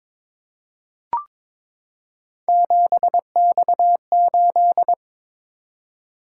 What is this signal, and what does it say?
Morse code, heard as a keyed tone, about 22 wpm, 700 Hz. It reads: 7X8
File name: practice / 22wpm / Three Letter-Number Combinations - 22wpm